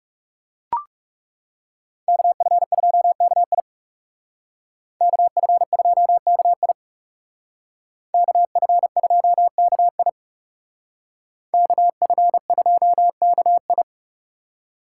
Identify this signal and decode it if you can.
Morse code, heard as a keyed tone, about 38 wpm, 700 Hz. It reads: KF2KI KF2KI KF2KI KF2KI